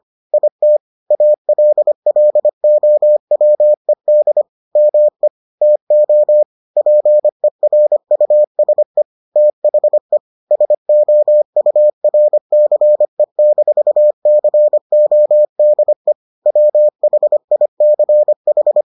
Morse code, heard as a keyed tone, about 25 wpm, 600 Hz. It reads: IT ALLOWED ME TO PERUSE THE SOURCE-CODE WHICH